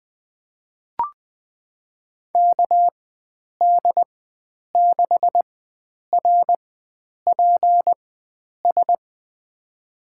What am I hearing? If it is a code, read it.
Morse code, heard as a keyed tone, about 20 wpm, spaced as for 9 wpm, 700 Hz. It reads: KD6RPS
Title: USA Call Signs - 4x Character Spacing - 20wpm